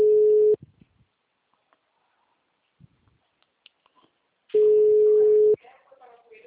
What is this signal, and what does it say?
Ringback tone (Europe, 425 Hz), with speech.